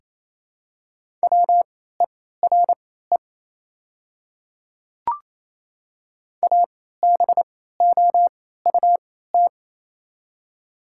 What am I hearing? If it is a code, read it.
Morse code, heard as a keyed tone, about 27 wpm, spaced as for 16 wpm, 700 Hz. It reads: WERE ABOUT
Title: 3x Character Spacing - Top 200 Words - 28wpm